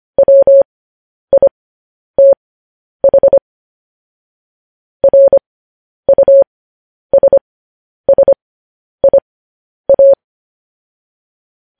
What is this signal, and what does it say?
Morse code, heard as a keyed tone, about 25 wpm, spaced as for 10 wpm, 550 Hz. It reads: WITH RUSSIA
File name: podcast / News headlines-1x-25x10